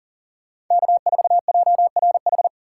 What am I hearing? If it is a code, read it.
Morse code, heard as a keyed tone, about 40 wpm, 700 Hz. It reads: K4JRH
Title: USA Call Signs - 3x Character Spacing - 40wpm